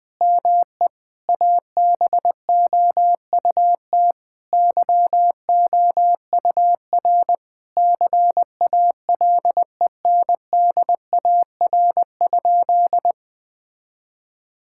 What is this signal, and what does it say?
Morse code, heard as a keyed tone, about 20 wpm, 700 Hz. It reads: ME ABOUT YOUR CALENDAR?
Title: Sentences from Top 1600 Words 20wpm